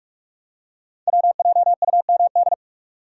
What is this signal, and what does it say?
Morse code, heard as a keyed tone, about 44 wpm, 690 Hz. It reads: WJUMD
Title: Code Groups - 5 Letters - 45wpm